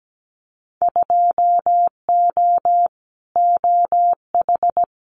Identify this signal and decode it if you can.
Morse code, heard as a keyed tone, about 17 wpm, 700 Hz. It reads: 2O OH
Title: Winter Field Day Exchanges 17wpm